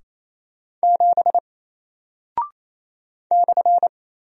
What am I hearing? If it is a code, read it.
Morse code, heard as a keyed tone, about 27 wpm, 700 Hz. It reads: 7 /
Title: Single Character (Letter:Number:Punctuation:Prosign) Alt 28wpm